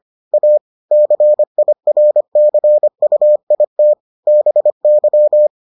Morse code, heard as a keyed tone, about 25 wpm, 600 Hz. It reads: A CIRCUIT BY